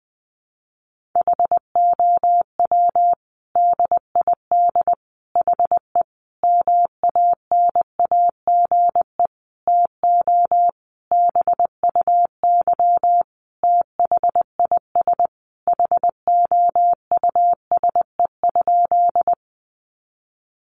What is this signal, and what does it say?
Morse code, heard as a keyed tone, about 19 wpm, 700 Hz. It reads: HOW DID HE MANAGE TO BUY THIS HOUSE?
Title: Sentences from Top 1400 Words 20wpm